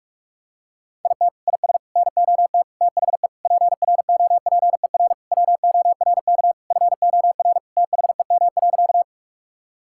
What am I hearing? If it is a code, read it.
Morse code, heard as a keyed tone, about 41 wpm, 700 Hz. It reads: IT IS NOT THE PROPER WORK FOR THEM.
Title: Sentences from Top 1000 Words - Review Part 5 - 45wpm